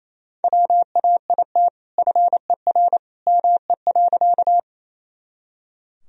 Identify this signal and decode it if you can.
Morse code, heard as a keyed tone, about 28 wpm, 700 Hz. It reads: WAIT FER ME.
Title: Sentences from Top 500 Words and Abbreviations 28wpm